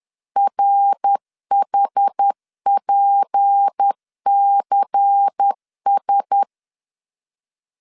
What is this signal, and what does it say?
Morse code, heard as a keyed tone, about 10 wpm, 800 Hz. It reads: RHPCS
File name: LEZIONI / 6 PKX0 / 6 P 10wpm 2432